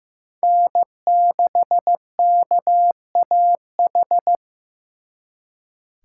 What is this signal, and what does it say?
Morse code, heard as a keyed tone, about 18 wpm, 700 Hz. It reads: N6KAH
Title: Speed Racing - USA 1x3 Call Signs 15wpm